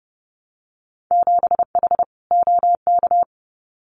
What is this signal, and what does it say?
Morse code, heard as a keyed tone, about 30 wpm, 700 Hz. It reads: 7H OK
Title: Winter Field Day Exchanges Encore 30wpm